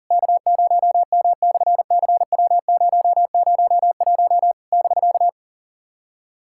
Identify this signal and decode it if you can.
Morse code, heard as a keyed tone, about 38 wpm, 700 Hz. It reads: K0M/CW001 <BK>
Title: SOTA QSO Practice 40wpm